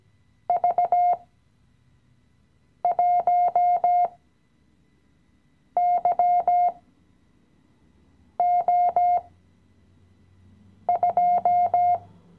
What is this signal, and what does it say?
Morse code, heard as a keyed tone, about 17 wpm, spaced as for 5 wpm, 710 Hz. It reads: V1YO2
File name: CW-Online-Kurs / Anfänger / Lektion 017